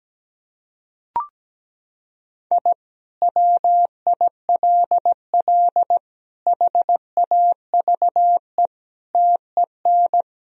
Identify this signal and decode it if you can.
Morse code, heard as a keyed tone, about 17 wpm, 700 Hz. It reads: I WILL HAVE TEN
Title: Sentences from Top 1000 Words - Review Part 2 - 17wpm